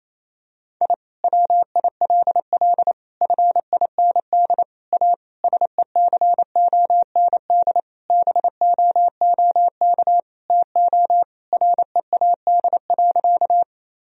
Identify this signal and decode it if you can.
Morse code, heard as a keyed tone, about 28 wpm, 700 Hz. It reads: I WILL FIND A SECOND BOOK TO READ.
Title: Sentences from Top 1000 Words - Review Part 1 - 28wpm